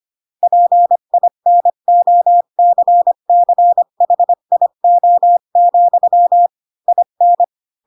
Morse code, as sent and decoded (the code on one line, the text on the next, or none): .--. .. -. --- -.-. -.-. .... .. --- --..-- / .. -.
PINOCCHIO, IN